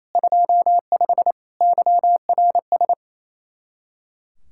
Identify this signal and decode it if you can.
Morse code, heard as a keyed tone, about 33 wpm, 700 Hz. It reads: 25 YRS
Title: Speed-Racing Common QSO Phrases Encore - 28wpm